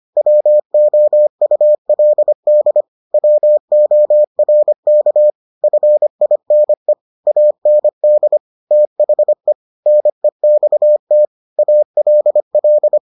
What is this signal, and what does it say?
Morse code, heard as a keyed tone, about 25 wpm, 600 Hz. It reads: WOULD WORK FINE AND THE NEXT ALL